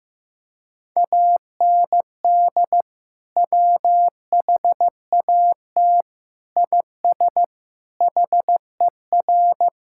Morse code, heard as a keyed tone, about 15 wpm, 700 Hz. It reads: AND WHAT IS HER
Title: Sentences from Top 1000 Words - Review Part 6 - 15wpm